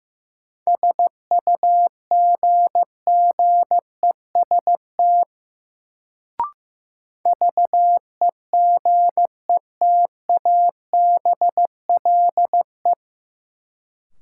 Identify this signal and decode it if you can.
Morse code, heard as a keyed tone, about 15 wpm, 700 Hz. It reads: SUGGEST VEGETABLE
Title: Sentences from Top 1300 Words - New Word Drill - 15wpm